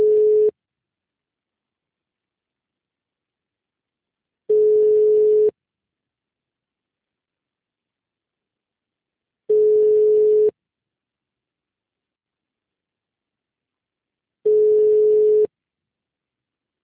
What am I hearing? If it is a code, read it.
Ringback tone (Europe, 425 Hz).
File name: out- -2011- - - .921